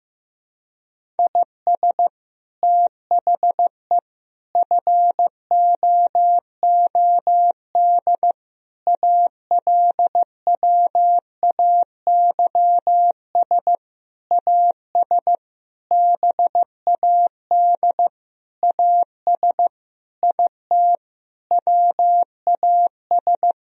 Morse code, heard as a keyed tone, about 15 wpm, 700 Hz. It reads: IS THE FOOD ALWAYS AS BAD AS IT WAS
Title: Sentences from Top 1000 Words - Review Part 1 - 15wpm